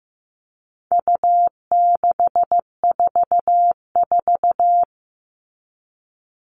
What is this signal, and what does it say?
Morse code, heard as a keyed tone, about 15 wpm, 700 Hz. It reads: U644
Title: All Cut Numbers - 4-Digit 15wpm